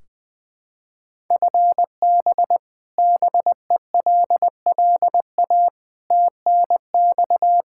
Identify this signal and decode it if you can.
Morse code, heard as a keyed tone, about 20 wpm, 700 Hz. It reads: FB BELLA TNX
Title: Practice QSO - Exchange 3 - More Info - 20wpm